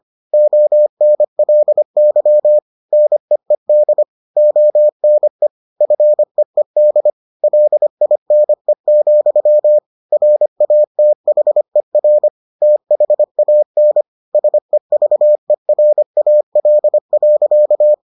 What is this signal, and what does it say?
Morse code, heard as a keyed tone, about 25 wpm, 600 Hz. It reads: ONLY NEED ONE FEED LINE, RATHER THAN SEVERAL.